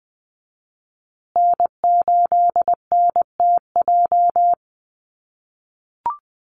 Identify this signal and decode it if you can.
Morse code, heard as a keyed tone, about 20 wpm, 700 Hz. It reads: N8NTJ
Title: USA Call Signs Repeated 3x 20wpm